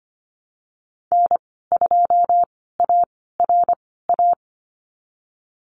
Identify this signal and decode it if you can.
Morse code, heard as a keyed tone, about 25 wpm, spaced as for 16 wpm, 700 Hz. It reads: N2ARA
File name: Lesson 15g - Number 5 - Call Signs 25wpm